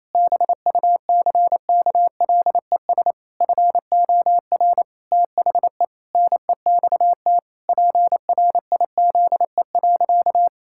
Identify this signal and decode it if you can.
Morse code, heard as a keyed tone, about 27 wpm, 700 Hz. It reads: BUCKLES FOR THE NEXT PRIZE.
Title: Pangram Challenge v2 - 28wpm